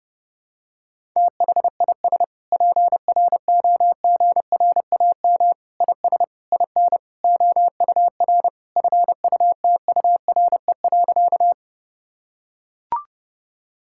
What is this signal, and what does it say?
Morse code, heard as a keyed tone, about 32 wpm, 700 Hz. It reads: THIS PROGRAM IS IN OUR FUTURE.
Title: Sentences from Top 1000 Words - Review Part 5 - 30wpm